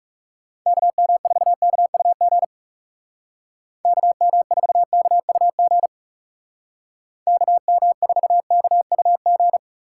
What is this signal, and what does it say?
Morse code, heard as a keyed tone, about 36 wpm, 700 Hz. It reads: KM4KUG KM4KUG KM4KUG
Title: Speed Racing - USA 2x3 Call Signs 30wpm